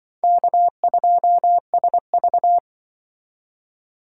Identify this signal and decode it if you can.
Morse code, heard as a keyed tone, about 24 wpm, 700 Hz. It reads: K2SV
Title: Speed Racing - USA 1x2 Call Signs 20wpm